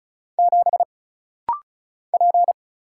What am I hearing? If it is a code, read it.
Morse code, heard as a keyed tone, about 35 wpm, 700 Hz. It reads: 7 P
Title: Single Letter-Number - Rapid-Fire 35wpm